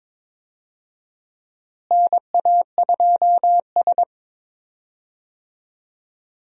Morse code, heard as a keyed tone, about 21 wpm, 700 Hz. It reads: NA2S
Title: USA Call Signs - 4 Characters - Repeated 3x - 22wpm